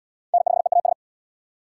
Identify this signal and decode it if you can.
Morse code, heard as a keyed tone, about 71 wpm, 700 Hz. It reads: N5EAS